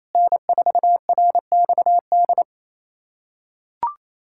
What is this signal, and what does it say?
Morse code, heard as a keyed tone, about 27 wpm, 700 Hz. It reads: N4RXD
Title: USA Callsign DE Callsign - 5 Characters - 28wpm